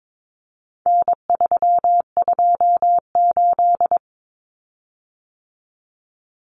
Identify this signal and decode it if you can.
Morse code, heard as a keyed tone, about 22 wpm, 700 Hz. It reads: N328
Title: Common Cut Numbers - 4-Digit 22wpm